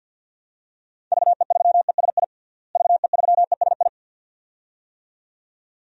Morse code, heard as a keyed tone, about 53 wpm, 700 Hz. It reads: VE3ESI VE3ESI
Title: DXCC-100 Call Signs - 6 Characters - Repeated 2x - 50wpm